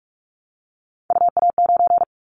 Morse code, heard as a keyed tone, about 44 wpm, 700 Hz. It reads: UA9
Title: Callsign Prefixes for Top 100 DXCC - 45wpm